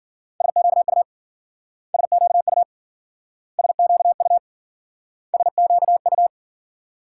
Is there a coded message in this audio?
Yes: Morse code from 0.4 to 6.3 s, heard as a keyed tone, about 53 wpm, 700 Hz. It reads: SQU SQU SQU SQU